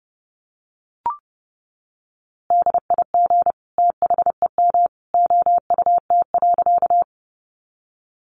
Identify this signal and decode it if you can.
Morse code, heard as a keyed tone, about 30 wpm, 700 Hz. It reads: DIG THEM OUT.
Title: A Sound Of Thunder 30wpm